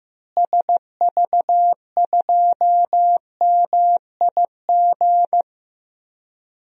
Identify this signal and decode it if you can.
Morse code, heard as a keyed tone, about 15 wpm, 700 Hz. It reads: SV2MIG
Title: DXCC-100 Call Signs - 6 Characters - Repeated 3x - 15wpm